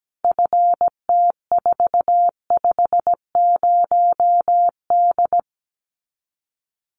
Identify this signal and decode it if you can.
Morse code, heard as a keyed tone, about 17 wpm, 700 Hz. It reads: FT450D
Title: Common QSO Phrases 17wpm